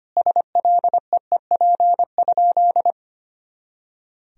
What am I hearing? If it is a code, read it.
Morse code, heard as a keyed tone, about 25 wpm, 700 Hz. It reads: SLEEP?
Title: Sentences from Top 500 Words and Abbreviations 25wpm